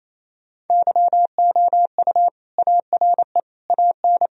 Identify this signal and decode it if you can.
Morse code, heard as a keyed tone, about 28 wpm, 700 Hz. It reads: YOU ARE AN
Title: Sentences from Top 1000 Words - Review Part 1 - 28wpm